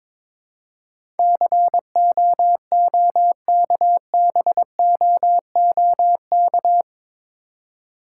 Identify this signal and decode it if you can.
Morse code, heard as a keyed tone, about 23 wpm, 700 Hz. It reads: COOKBOOK
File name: Difficult Words with Hints - 22wpm